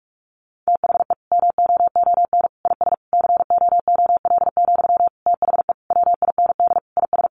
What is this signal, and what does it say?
Morse code, heard as a keyed tone, about 44 wpm, 700 Hz. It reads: THE MOON IS COOL, THE WIND IS
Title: Sentences from Top 1200 Words 45wpm